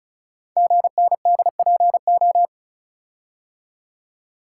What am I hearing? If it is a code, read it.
Morse code, heard as a keyed tone, about 36 wpm, 690 Hz. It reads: GNDPO
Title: Code Groups - 5 Letters - 35wpm